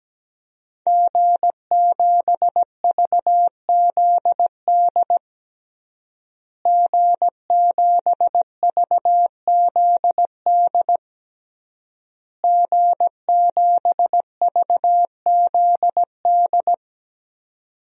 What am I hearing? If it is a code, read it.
Morse code, heard as a keyed tone, about 17 wpm, 700 Hz. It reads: G7VZD G7VZD G7VZD
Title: DXCC-100 Call Signs - 5 Characters - Repeated 3x - 17wpm